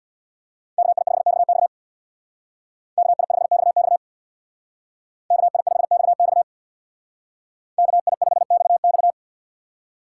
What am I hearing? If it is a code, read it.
Morse code, heard as a keyed tone, about 64 wpm, 700 Hz. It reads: KI5XX KI5XX KI5XX KI5XX